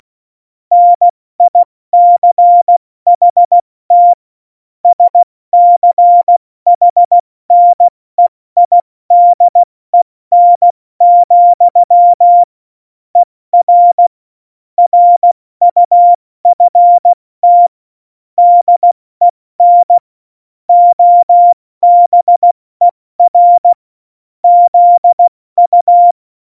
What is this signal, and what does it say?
Morse code, heard as a keyed tone, about 16 wpm, 700 Hz. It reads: NICHT SCHNEIDEN, ER RUFT DEN OBER ZU
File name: Morsekurs / Witze / Witz 26. November 2019, 70 F80